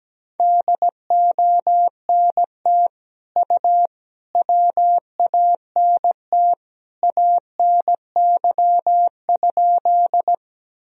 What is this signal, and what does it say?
Morse code, heard as a keyed tone, about 17 wpm, 700 Hz. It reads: DONT U WANT ANY?
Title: Sentences from Top 300 Words and Abbreviations 17wpm